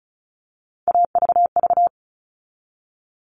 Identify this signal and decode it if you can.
Morse code, heard as a keyed tone, about 34 wpm, 700 Hz. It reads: AVV
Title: All Cut Numbers - 3-Digit 35wpm